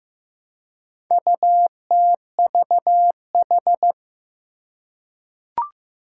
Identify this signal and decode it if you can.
Morse code, heard as a keyed tone, about 15 wpm, 700 Hz. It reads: UTVH
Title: Four Challenging Letter-Number Combinations 15wpm